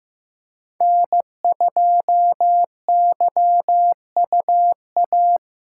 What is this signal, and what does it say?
Morse code, heard as a keyed tone, about 15 wpm, 700 Hz. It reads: N2YUA
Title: USA Call Signs - 5 Characters - 3x Character Spacing - 15wpm